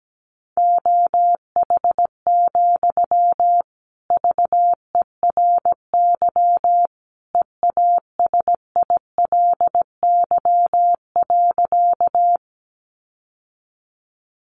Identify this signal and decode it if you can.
Morse code, heard as a keyed tone, about 17 wpm, 700 Hz. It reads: OH, VERY EASILY.
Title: Sentences from Top 900 Words 17wpm